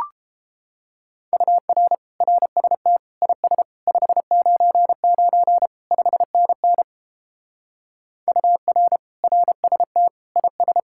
Morse code, heard as a keyed tone, about 31 wpm, 700 Hz. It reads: UR RST IS 599 5NN UR RST IS